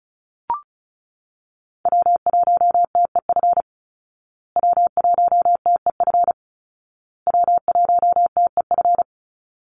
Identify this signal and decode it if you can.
Morse code, heard as a keyed tone, about 34 wpm, 700 Hz. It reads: W1TEF W1TEF W1TEF
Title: Call Signs Repeated 2x - 35wpm